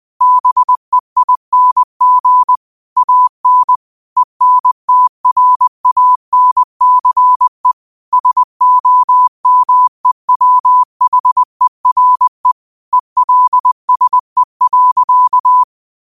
Morse code, heard as a keyed tone, about 20 wpm, 1000 Hz. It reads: BEING AN ENTRANCE SOMEWHERE ELSE.
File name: 20wpm